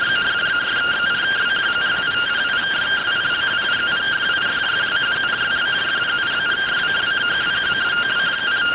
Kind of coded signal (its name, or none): two-tone teleprinter signal